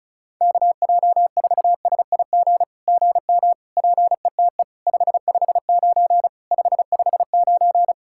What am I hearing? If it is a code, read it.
Morse code, heard as a keyed tone, about 36 wpm, 700 Hz. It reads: KJ4SIG GM PETE 559 559
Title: SOTA QSO Practice 35wpm